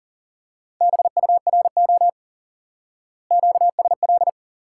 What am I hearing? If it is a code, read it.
Morse code, heard as a keyed tone, about 40 wpm, 700 Hz. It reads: BURO QSL
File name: course / instant-qso-element-course-3 / IQE Course 3 - Lesson 89 - YAGI - Two Elements - 40wpm